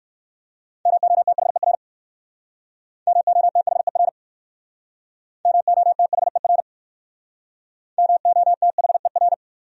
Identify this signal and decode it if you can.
Morse code, heard as a keyed tone, about 59 wpm, 700 Hz. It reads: MOTHER MOTHER MOTHER MOTHER